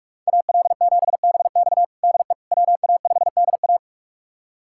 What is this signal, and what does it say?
Morse code, heard as a keyed tone, about 41 wpm, 700 Hz. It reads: AL7BX DE WA5DA
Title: USA Callsign DE Callsign - 5 Characters - 45wpm